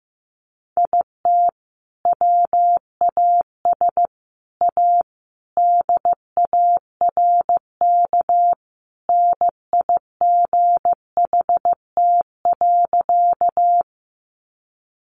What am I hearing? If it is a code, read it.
Morse code, heard as a keyed tone, about 15 wpm, 700 Hz. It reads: IT WAS A DARK NIGHT.
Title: Sentences from Top 500 Words 15wpm